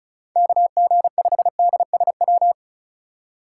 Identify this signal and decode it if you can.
Morse code, heard as a keyed tone, about 36 wpm, 700 Hz. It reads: KG5DSW
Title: USA Call Signs - 4x Character Spacing - 35wpm